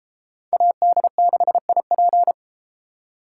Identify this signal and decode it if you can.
Morse code, heard as a keyed tone, about 32 wpm, 700 Hz. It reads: AD6IP